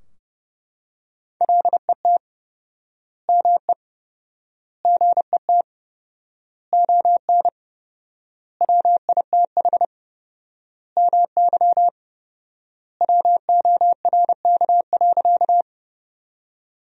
Morse code, heard as a keyed tone, about 30 wpm, 700 Hz. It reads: LET ME GET ON WITH MY WORK.
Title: 4x Word-Spacing - Sentences from Top 300 Words - 30wpm